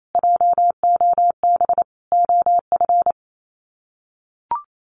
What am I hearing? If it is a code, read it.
Morse code, heard as a keyed tone, about 28 wpm, 700 Hz. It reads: JOB OF
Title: Sets of 5 Words Encore2 28wpm